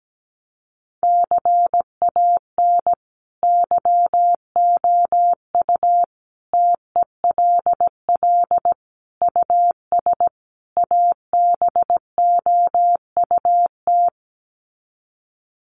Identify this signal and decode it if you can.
Morse code, heard as a keyed tone, about 17 wpm, 700 Hz. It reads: CAN YOU TELL US ABOUT
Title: Sets of 5 Words Encore2 17wpm